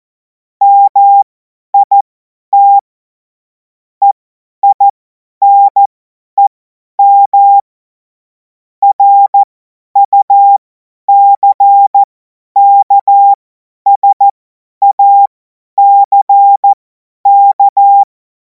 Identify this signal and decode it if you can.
Morse code, heard as a keyed tone, about 14 wpm, spaced as for 10 wpm, 800 Hz. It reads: MIT EINEM RUCKSACK
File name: Morsekurs / Witze / Witz 16. Juni 2026 50 F70